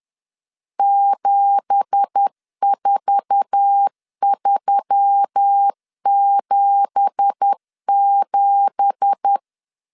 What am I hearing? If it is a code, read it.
Morse code, heard as a keyed tone, about 10 wpm, 800 Hz. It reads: 74377
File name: LEZIONI / 9 2867 10wpm 2733